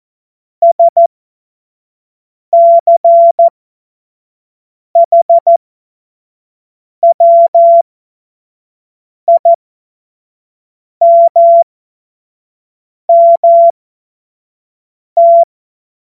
Morse code, heard as a keyed tone, about 14 wpm, spaced as for 5 wpm, 680 Hz. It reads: SCHWIMMT
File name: Morsekurs / Witze / Witz 21. Dezember 2021, 25 F70